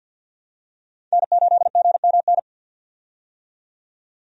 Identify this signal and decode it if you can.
Morse code, heard as a keyed tone, about 51 wpm, 700 Hz. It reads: N8GMN
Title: USA Call Signs - 5 Characters - 50wpm